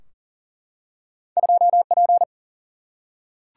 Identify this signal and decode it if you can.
Morse code, heard as a keyed tone, about 40 wpm, 700 Hz. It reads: 2P
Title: Single-Letter Single-Number 40wpm